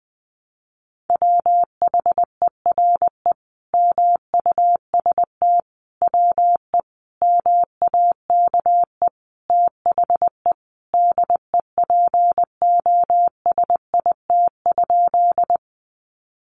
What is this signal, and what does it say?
Morse code, heard as a keyed tone, about 20 wpm, 700 Hz. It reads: WHERE MUST WE MAKE THE DEPOSIT?
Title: Sentences from Top 1300 Words 20wpm